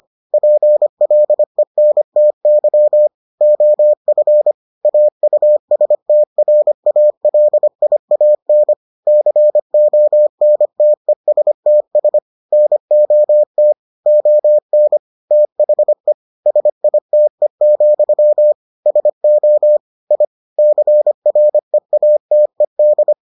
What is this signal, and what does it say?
Morse code, heard as a keyed tone, about 25 wpm, 600 Hz. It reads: PLENTY OF AUSTRALIAN CONTESTS NOT ON THE SITE, SO I CREATED